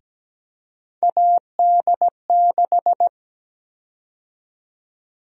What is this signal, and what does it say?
Morse code, heard as a keyed tone, about 17 wpm, 700 Hz. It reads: AD6
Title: USA Call Sign Prefixes - Repeated 2x - 17wpm